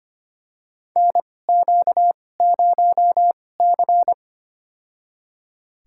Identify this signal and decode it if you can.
Morse code, heard as a keyed tone, about 25 wpm, spaced as for 18 wpm, 700 Hz. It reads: NQ0C
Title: USA Call Signs - 4 Characters - 2x Character Spacing - 25wpm